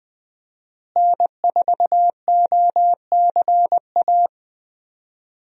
Morse code, heard as a keyed tone, about 20 wpm, 700 Hz. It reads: N4OCA
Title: USA Call Signs - 5 Characters - Repeated 3x - 20wpm